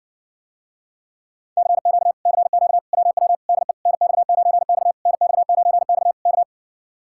Morse code, heard as a keyed tone, about 59 wpm, 700 Hz. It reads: CQ CQ WW DE NX9X NX9X K